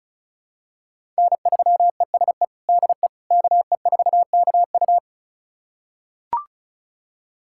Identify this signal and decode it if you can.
Morse code, heard as a keyed tone, about 34 wpm, 700 Hz. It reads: N3ESE DE KE4KU
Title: USA Callsign DE Callsign - 5 Characters - 35wpm